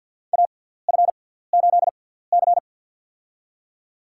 Morse code, heard as a keyed tone, about 52 wpm, spaced as for 17 wpm, 700 Hz. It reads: AF8C
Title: USA Call Signs - 4 Characters - 6x Character Spacing - 50wpm